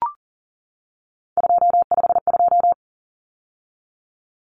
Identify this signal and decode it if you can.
Morse code, heard as a keyed tone, about 40 wpm, 700 Hz. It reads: 252
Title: Numbers - 3-Digit 40wpm